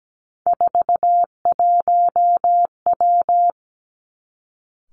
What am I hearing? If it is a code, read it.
Morse code, heard as a keyed tone, about 17 wpm, 700 Hz. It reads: 41W
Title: Three Challenging Letter-Number Combinations 17wpm